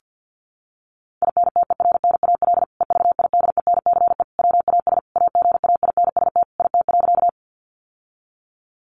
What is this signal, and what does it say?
Morse code, heard as a keyed tone, about 50 wpm, 700 Hz. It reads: INTERNAL EVIDENCE WAS AGAINST IT.
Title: Sentences from Top 1400 Words 50wpm